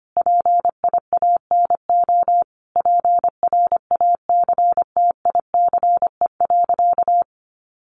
Morse code, heard as a keyed tone, about 25 wpm, 700 Hz. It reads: PIANO PRACTICE.
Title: Additional Easy Phrases 25wpm